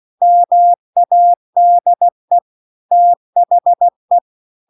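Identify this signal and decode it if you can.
Morse code, heard as a keyed tone, about 16 wpm, 700 Hz. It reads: MADE THE
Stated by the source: Morse practice text at 16 WPM.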